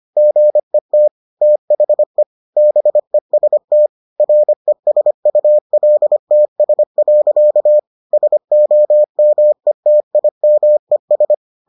Morse code, heard as a keyed tone, about 25 wpm, 600 Hz. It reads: GET THE BEST RESULTS. SOMETIMES